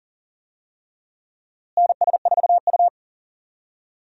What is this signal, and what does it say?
Morse code, heard as a keyed tone, about 40 wpm, 700 Hz. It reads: NS4U
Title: USA Call Signs - 4 Characters - Repeated 3x - 40wpm